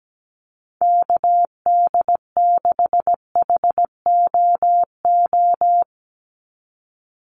Morse code, heard as a keyed tone, about 17 wpm, 700 Hz. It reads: KD6HOO
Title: USA Call Signs 17wpm